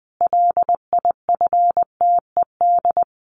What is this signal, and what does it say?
Morse code, heard as a keyed tone, about 20 wpm, 700 Hz. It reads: LIFTED
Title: A Sound Of Thunder 20wpm